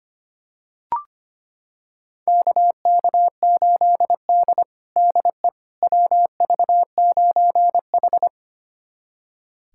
Morse code, heard as a keyed tone, about 25 wpm, 700 Hz. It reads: KK8D DE WV9H
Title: USA Callsign DE Callsign - 4 Characters - 25wpm